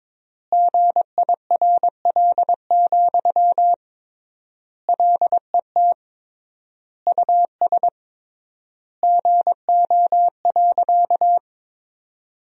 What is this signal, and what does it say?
Morse code, heard as a keyed tone, about 22 wpm, 700 Hz. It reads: GIRL, LET US GO.
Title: 3x Word-Spacing - Sentences from Top 400 Words - 22wpm